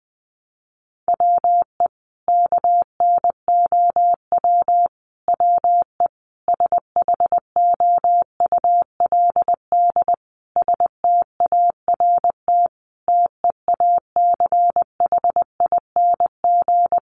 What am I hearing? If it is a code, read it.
Morse code, heard as a keyed tone, about 20 wpm, 700 Hz. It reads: WE KNOW WE SHOULD START TEACHING
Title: Sentences from Top 1100 Words 20wpm